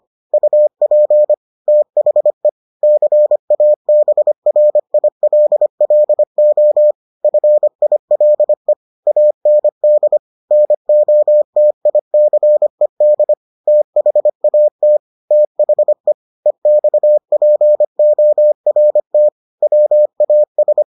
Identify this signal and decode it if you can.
Morse code, heard as a keyed tone, about 25 wpm, 600 Hz. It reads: UP THE CABRILLO FILE AND NOTICED THAT THE EXPORT WAS